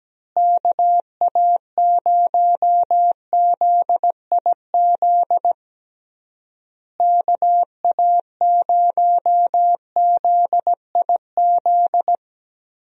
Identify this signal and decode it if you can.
Morse code, heard as a keyed tone, about 17 wpm, 700 Hz. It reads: KA0ZIZ KA0ZIZ
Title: USA Call Signs - 6 Characters - Repeated 3x - 17wpm